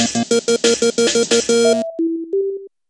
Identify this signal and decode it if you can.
There is no code.